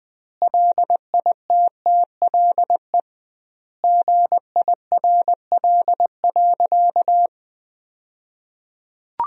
Morse code, heard as a keyed tone, about 20 wpm, 700 Hz. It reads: LITTLE GIRL.
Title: 2x Word-Spacing - Sentences from Top 400 Words - 20wpm